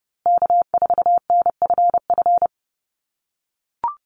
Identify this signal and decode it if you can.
Morse code, heard as a keyed tone, about 30 wpm, 700 Hz. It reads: K4NFF
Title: USA Call Signs Repeated 2x 30wpm